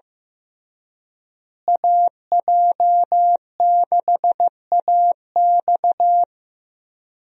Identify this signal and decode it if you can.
Morse code, heard as a keyed tone, about 15 wpm, 700 Hz. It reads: AJ6AX
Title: USA Call Signs - 5 Characters - Repeated 3x - 15wpm